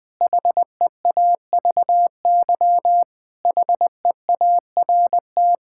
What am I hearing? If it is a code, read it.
Morse code, heard as a keyed tone, about 20 wpm, 700 Hz. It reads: HEAVY HEART
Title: Sentences with 7-Word Phrases - 20wpm